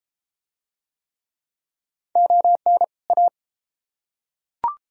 Morse code, heard as a keyed tone, about 31 wpm, 700 Hz. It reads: ON A